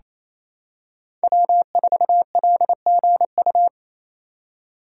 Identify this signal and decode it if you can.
Morse code, heard as a keyed tone, about 27 wpm, 700 Hz. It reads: W4LGU
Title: Code Groups - 5 Letter:Numbers - 28wpm